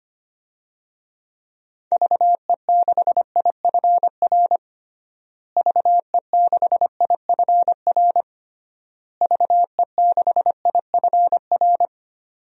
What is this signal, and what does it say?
Morse code, heard as a keyed tone, about 25 wpm, 700 Hz. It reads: VE6IFR VE6IFR VE6IFR
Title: DXCC-100 Call Signs - 6 Characters - Repeated 3x - 25wpm